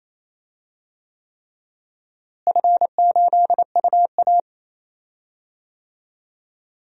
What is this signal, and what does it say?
Morse code, heard as a keyed tone, about 27 wpm, 700 Hz. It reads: F8UA
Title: DXCC-100 Call Signs - 4 Characters - Repeated 3x - 28wpm